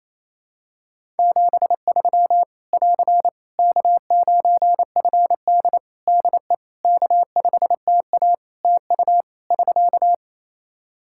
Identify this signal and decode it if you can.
Morse code, heard as a keyed tone, about 28 wpm, 700 Hz. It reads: 73 <AR> K9FD DE K5TA TU <SK>
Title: K1USN SST QSO Practice 28wpm